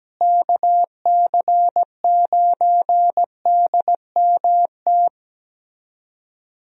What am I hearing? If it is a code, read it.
Morse code, heard as a keyed tone, about 17 wpm, 700 Hz. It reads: KC9DMT
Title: USA Call Signs - 6 Characters - Repeated 2x - 17wpm